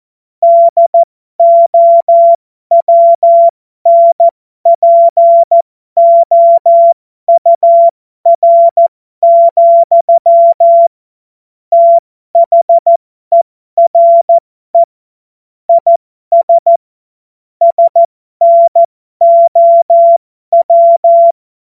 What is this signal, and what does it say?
Morse code, heard as a keyed tone, about 14 wpm, spaced as for 12 wpm, 680 Hz. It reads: DOWNPOUR, THERE IS SNOW